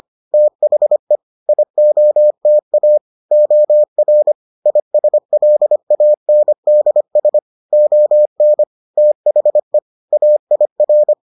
Morse code, heard as a keyed tone, about 25 wpm, 600 Hz. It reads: THE IOTA OR ISLANDS ON THE AIR